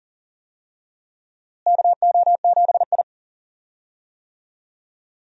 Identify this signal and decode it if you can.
Morse code, heard as a keyed tone, about 40 wpm, 700 Hz. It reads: KO7I
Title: USA Call Signs - 4 Characters - Repeated 2x - 40wpm